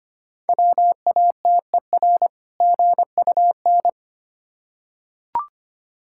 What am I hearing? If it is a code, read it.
Morse code, heard as a keyed tone, about 25 wpm, 700 Hz. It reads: WATER GUN
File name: practice / 25wpm / Difficult Words with Hints - 25wpm